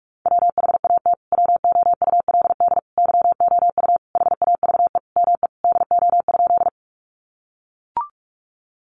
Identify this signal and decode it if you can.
Morse code, heard as a keyed tone, about 44 wpm, 700 Hz. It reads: WHAT WOULD YOU HAVE ME DO?
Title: Sentences from Top 200 Words 45wpm